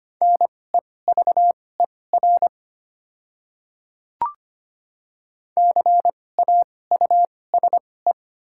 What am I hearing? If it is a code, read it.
Morse code, heard as a keyed tone, about 25 wpm, spaced as for 18 wpm, 700 Hz. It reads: NEVER CAUSE
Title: 2x Character Spacing - Top 300 Words - 25wpm